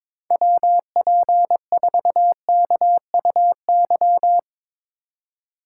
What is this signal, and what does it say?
Morse code, heard as a keyed tone, about 22 wpm, 700 Hz. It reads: WP4KUY
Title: USA Call Signs - 6 Characters - 4x Character Spacing - 22wpm